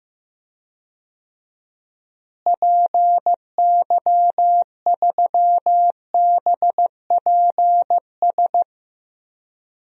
Morse code, heard as a keyed tone, about 15 wpm, 700 Hz. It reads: PY3BPS
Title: DXCC-100 Call Signs - 6 Characters - Repeated 2x - 15wpm